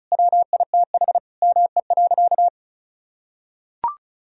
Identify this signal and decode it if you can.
Morse code, heard as a keyed tone, about 36 wpm, 700 Hz. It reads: WITH ME.
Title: Sentences from Top 1000 Words - Review Part 1 - 35wpm